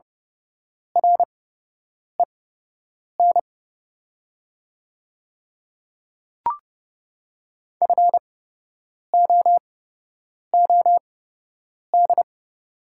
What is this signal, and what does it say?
Morse code, heard as a keyed tone, about 30 wpm, spaced as for 8 wpm, 700 Hz. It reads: REN FOOD
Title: 8x Character Spacing - Top 300 Words - 30wpm